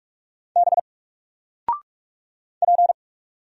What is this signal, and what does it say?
Morse code, heard as a keyed tone, about 44 wpm, 700 Hz. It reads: B P
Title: Single Letter-Number Alt 45wpm